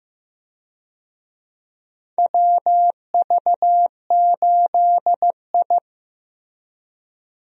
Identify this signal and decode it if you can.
Morse code, heard as a keyed tone, about 15 wpm, 700 Hz. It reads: WV8I
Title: USA Call Signs - 4 Characters - Repeated 3x - 15wpm